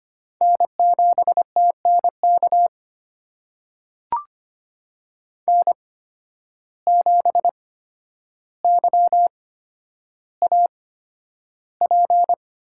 Morse code, heard as a keyed tone, about 25 wpm, spaced as for 11 wpm, 700 Hz. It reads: N7TNK N7YAP
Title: USA Call Signs - 5 Characters - 8x Character Spacing - 25wpm